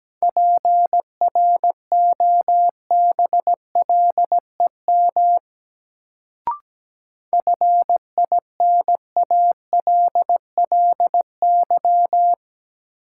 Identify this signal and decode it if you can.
Morse code, heard as a keyed tone, about 17 wpm, 700 Hz. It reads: PROBLEM FINALLY
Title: Sentences from Top 1000 Words - New Word Drill - 17wpm